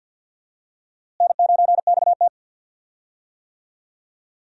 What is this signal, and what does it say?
Morse code, heard as a keyed tone, about 51 wpm, 700 Hz. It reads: N9XT
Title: USA Call Signs - 4 Characters - Repeated 3x - 50wpm